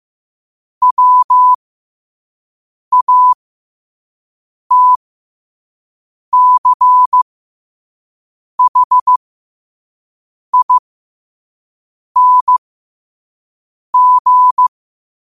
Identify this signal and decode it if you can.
Morse code, heard as a keyed tone, about 15 wpm, spaced as for 5 wpm, 1000 Hz. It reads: WATCHING
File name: qotdfiles / 5wpm